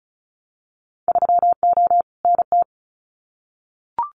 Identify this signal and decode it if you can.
Morse code, heard as a keyed tone, about 34 wpm, 700 Hz. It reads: 3O NT
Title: Winter Field Day Exchanges Encore 35wpm